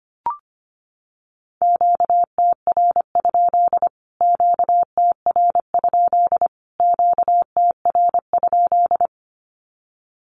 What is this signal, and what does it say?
Morse code, heard as a keyed tone, about 25 wpm, 700 Hz. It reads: QTR? QTR? QTR?
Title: Q-Codes Repeated 3x 25wpm